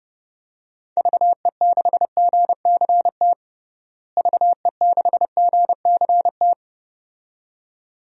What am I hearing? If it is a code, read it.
Morse code, heard as a keyed tone, about 30 wpm, 700 Hz. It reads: VE6GCT VE6GCT
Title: DXCC-100 Call Signs - 6 Characters - Repeated 2x - 30wpm